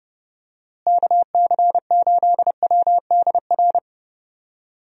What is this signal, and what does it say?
Morse code, heard as a keyed tone, about 30 wpm, 700 Hz. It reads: KC8WDR
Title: USA Call Signs - 6 Characters - Repeated 2x - 30wpm